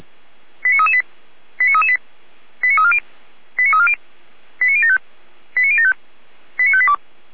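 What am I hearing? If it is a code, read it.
Five-tone selective call (ZVEI), heard as quick runs of five tones, tones of about 70 ms. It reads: 89338 89338 89340 89340 89075 89075 89582
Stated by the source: ZVEI five-tone selective call received via SDR, hosted by Feuerwehr Orlamünde (fire brigade).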